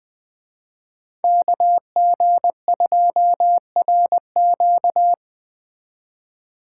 Morse code, heard as a keyed tone, about 20 wpm, 700 Hz. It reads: KG2RQ